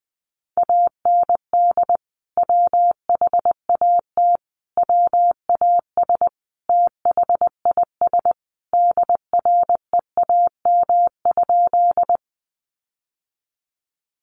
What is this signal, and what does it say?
Morse code, heard as a keyed tone, about 20 wpm, 700 Hz. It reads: AND WHAT WAS THIS DREAM?
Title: Sentences from Top 1500 Words 20wpm